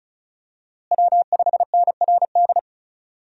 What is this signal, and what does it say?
Morse code, heard as a keyed tone, about 34 wpm, 700 Hz. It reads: W5NRD
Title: USA Call Signs - 5 Characters - Repeated 2x - 35wpm